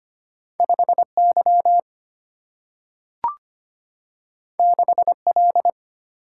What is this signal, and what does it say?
Morse code, heard as a keyed tone, about 25 wpm, 700 Hz. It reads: 5Y 6L
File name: Two Letter-Number Combinations - 25wpm